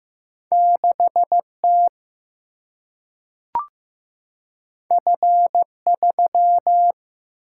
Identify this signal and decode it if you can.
Morse code, heard as a keyed tone, about 15 wpm, 700 Hz. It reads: 6 T F 3
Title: Single-Letter Single-Number 15wpm